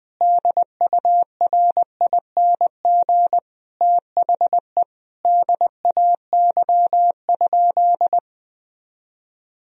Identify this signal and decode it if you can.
Morse code, heard as a keyed tone, about 21 wpm, 700 Hz. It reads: DURING THE DAY?
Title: Sentences from Top 1000 Words - Review Part 4 - 20wpm